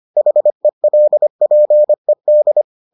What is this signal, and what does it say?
Morse code, heard as a keyed tone, about 25 wpm, 600 Hz. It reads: HELPED